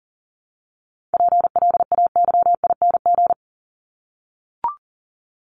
Morse code, heard as a keyed tone, about 40 wpm, 700 Hz. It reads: PLAYING
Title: ING Suffix - Top 100 Words with Base Words 40wpm